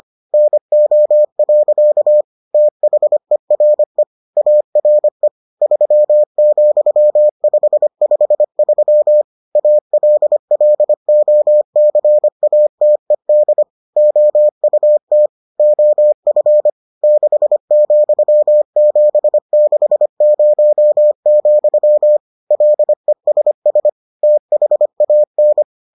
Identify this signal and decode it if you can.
Morse code, heard as a keyed tone, about 25 wpm, 600 Hz. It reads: NO. THERE ARE 3,553 ALLOCATED OUT OF 6,760, LESS THAN